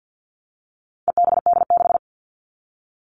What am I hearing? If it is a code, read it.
Morse code, heard as a keyed tone, about 50 wpm, 700 Hz. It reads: EBD6
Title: All Cut Numbers - 4-Digit 50wpm